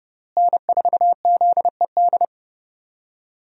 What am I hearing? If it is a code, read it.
Morse code, heard as a keyed tone, about 32 wpm, 700 Hz. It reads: N4ZED
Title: USA Call Signs - 5 Characters - Repeated 3x - 30wpm